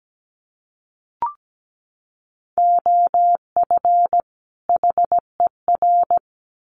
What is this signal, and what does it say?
Morse code, heard as a keyed tone, about 17 wpm, 700 Hz. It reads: OF HER
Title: Sets of 2 Words 17wpm